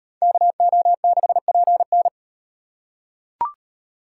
Morse code, heard as a keyed tone, about 37 wpm, 700 Hz. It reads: KO6PN